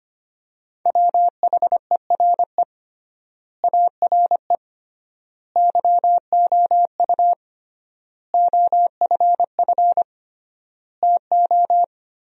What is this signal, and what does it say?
Morse code, heard as a keyed tone, about 25 wpm, 700 Hz. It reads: WHERE ARE YOU OFF TO
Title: 3x Word-Spacing - Sentences from Top 400 Words - 25wpm